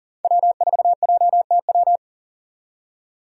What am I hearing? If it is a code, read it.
Morse code, heard as a keyed tone, about 40 wpm, 700 Hz. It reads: W4JTW
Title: USA Call Signs - 5 Characters - 40wpm